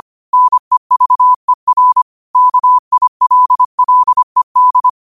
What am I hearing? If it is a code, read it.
Morse code, heard as a keyed tone, about 25 wpm, 1000 Hz. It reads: NEVER KILLED
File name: qotdfiles / 25wpm